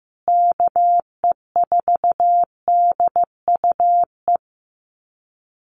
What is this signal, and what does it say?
Morse code, heard as a keyed tone, about 15 wpm, 700 Hz. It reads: KE4DUE
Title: USA Call Signs Repeated 2x 15wpm